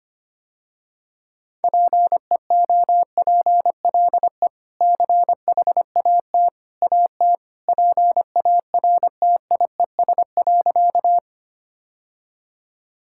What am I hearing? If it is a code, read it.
Morse code, heard as a keyed tone, about 25 wpm, 700 Hz. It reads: PEOPLE CHAT AT PARTIES.
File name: Easy Sentences with Hints - 25wpm